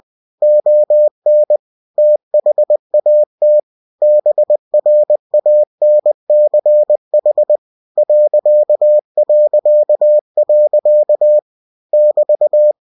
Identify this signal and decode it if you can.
Morse code, heard as a keyed tone, about 20 wpm, 600 Hz. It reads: ON THAT BRANCH ... <BT>
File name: cw-books / winnie-the-pooh / winnie-20wpm-page-0003